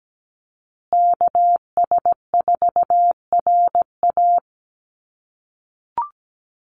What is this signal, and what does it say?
Morse code, heard as a keyed tone, about 17 wpm, 700 Hz. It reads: KS4RA
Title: USA Call Signs Repeated 2x 17wpm